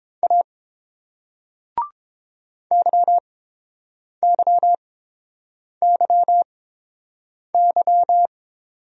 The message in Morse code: A YYYY

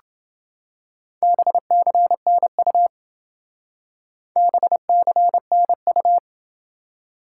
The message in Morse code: BCNU BCNU

